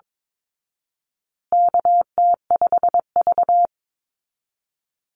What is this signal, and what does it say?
Morse code, heard as a keyed tone, about 22 wpm, 700 Hz. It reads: KT5V
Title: Call Signs Repeated 2x - 22wpm